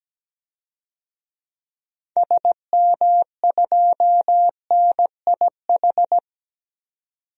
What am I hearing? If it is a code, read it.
Morse code, heard as a keyed tone, about 17 wpm, 700 Hz. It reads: SM2NIH
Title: DXCC-100 Call Signs - 6 Characters - 17wpm